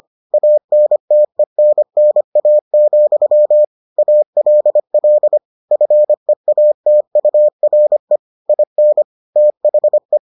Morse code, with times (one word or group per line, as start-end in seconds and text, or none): 0.3-3.6 ANTENNA,
4.0-5.4 ALL
5.7-8.2 FEATURE
8.5-9.0 IN
9.4-10.2 THE